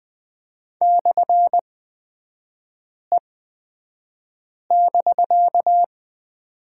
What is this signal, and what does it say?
Morse code, heard as a keyed tone, about 20 wpm, spaced as for 5 wpm, 700 Hz. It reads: /E<BK>
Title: 250 Lesson 40e - BK - Three Characters 20wpm